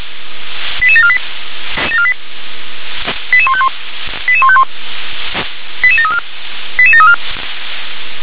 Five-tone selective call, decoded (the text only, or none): 88748 88151 88151 88734 88734